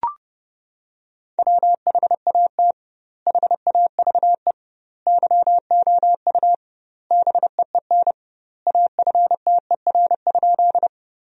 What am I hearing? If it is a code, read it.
Morse code, heard as a keyed tone, about 30 wpm, 700 Hz. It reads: WHAT HAVE YOU BEEN AFTER?
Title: 2x Word-Spacing - Sentences from Top 200 Words - 30wpm